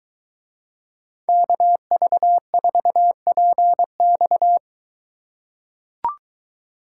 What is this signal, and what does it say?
Morse code, heard as a keyed tone, about 21 wpm, 700 Hz. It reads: KV4PX